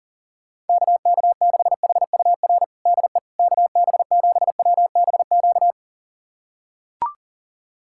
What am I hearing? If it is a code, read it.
Morse code, heard as a keyed tone, about 37 wpm, 700 Hz. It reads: KK6HUR DE KB7WBQ
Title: USA Callsign DE Callsign - 6 Characters - 40wpm